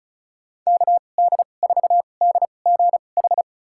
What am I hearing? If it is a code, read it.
Morse code, heard as a keyed tone, about 34 wpm, spaced as for 25 wpm, 700 Hz. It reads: KD4DGH
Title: USA Call Signs - 6 Characters - 2x Character Spacing - 35wpm